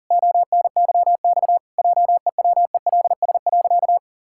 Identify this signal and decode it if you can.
Morse code, heard as a keyed tone, about 40 wpm, 700 Hz. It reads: ONYX JEWELS.
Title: Pangram Challenge v2 - 40wpm